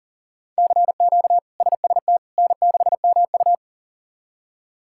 Morse code, heard as a keyed tone, about 40 wpm, 700 Hz. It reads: CQ SST N6MU
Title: K1USN SST QSO Practice - 40wpm